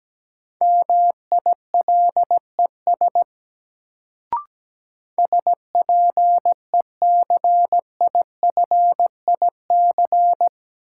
Morse code, heard as a keyed tone, about 17 wpm, 700 Hz. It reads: MILES SPECIFIC
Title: Sentences from Top 1000 Words - Review Part 3 - New Word Drill - 17wpm